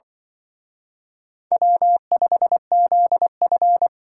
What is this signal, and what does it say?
Morse code, heard as a keyed tone, about 24 wpm, 700 Hz. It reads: W5ZF
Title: Speed Racing - USA 1x2 Call Signs 20wpm